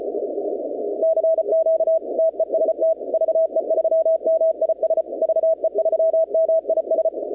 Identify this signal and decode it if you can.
Morse code, heard as a keyed tone, about 33 wpm, 660 Hz. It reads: CQ TEST VE3MIS VE3MIS
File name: contests / 2019-cw-wpx-kt5j-k5pi- -2330 0-21